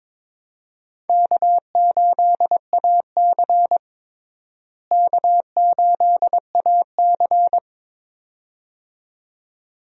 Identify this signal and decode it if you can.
Morse code, heard as a keyed tone, about 22 wpm, 700 Hz. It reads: K8AC K8AC
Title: USA Call Signs - 4 Characters - Repeated 2x - 22wpm